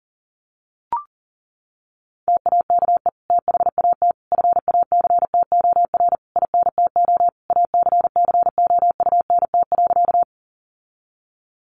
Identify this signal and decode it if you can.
Morse code, heard as a keyed tone, about 40 wpm, 700 Hz. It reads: TAKE THAT FACTOR INTO ACCOUNT.
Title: Sentences from Top 1300 Words 40wpm